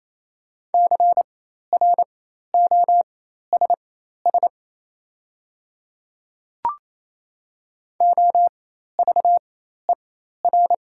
Morse code, heard as a keyed tone, about 28 wpm, spaced as for 13 wpm, 700 Hz. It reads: CROSS OVER
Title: 4x Character Spacing - Top 500 Words - 28wpm